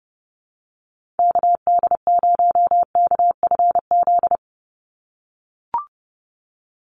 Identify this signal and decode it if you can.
Morse code, heard as a keyed tone, about 30 wpm, 700 Hz. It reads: KD0KFZ
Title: USA Call Signs 30wpm